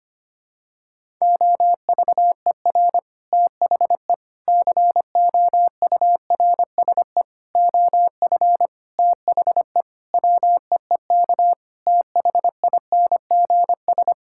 Morse code, heard as a keyed tone, about 25 wpm, 700 Hz. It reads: OVER THE COURSE OF THE WEEK THINGS
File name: practice / 25wpm / Sentences with 4-Word Phrases - 25wpm